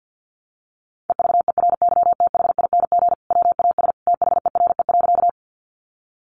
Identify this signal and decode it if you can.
Morse code, heard as a keyed tone, about 50 wpm, 700 Hz. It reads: EVERYTHING WAS THERE.
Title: Sentences from Top 700 Words 50wpm